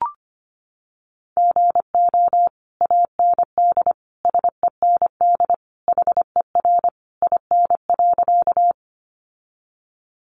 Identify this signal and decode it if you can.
Morse code, heard as a keyed tone, about 25 wpm, 700 Hz. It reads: GO AND SEND HER IN.
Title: Sentences from Top 1500 Words 25wpm